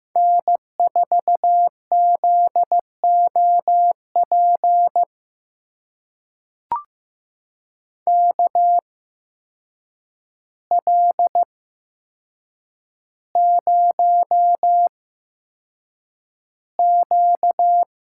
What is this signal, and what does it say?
Morse code, heard as a keyed tone, about 15 wpm, spaced as for 7 wpm, 700 Hz. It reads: N4ZOP KL0Q
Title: USA Call Signs - 8x Character Spacing - 15wpm